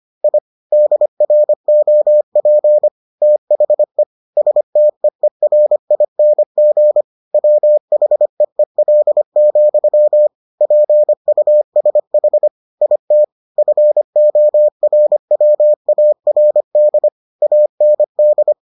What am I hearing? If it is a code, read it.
Morse code, heard as a keyed tone, about 25 wpm, 600 Hz. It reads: I DROP THE STEERING WHEEL, PUSH IT FORWARD AND